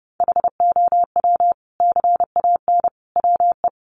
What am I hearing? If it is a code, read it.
Morse code, heard as a keyed tone, about 30 wpm, 700 Hz. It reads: HOW CAN WE
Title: Sentences from Top 1400 Words 30wpm